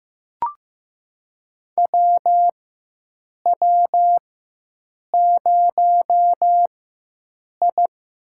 Morse code, heard as a keyed tone, about 15 wpm, spaced as for 7 wpm, 700 Hz. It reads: WW0I
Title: USA Call Signs - 4 Characters - 4x Character Spacing - 15wpm